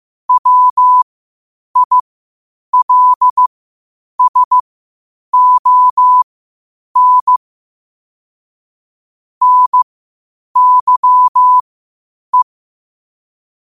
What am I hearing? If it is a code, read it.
Morse code, heard as a keyed tone, about 15 wpm, spaced as for 8 wpm, 1000 Hz. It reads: WILSON NYE
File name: qotdfiles / 8wpm